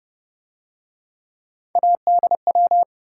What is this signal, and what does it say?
Morse code, heard as a keyed tone, about 29 wpm, 700 Hz. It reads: ADW